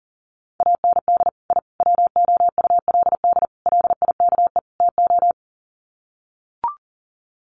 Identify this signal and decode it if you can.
Morse code, heard as a keyed tone, about 40 wpm, 700 Hz. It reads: AND I WOULD LIKE TO
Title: Sets of 5 Words Encore 40wpm